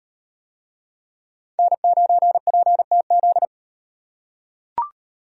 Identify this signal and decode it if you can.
Morse code, heard as a keyed tone, about 38 wpm, 700 Hz. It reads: N9PTZ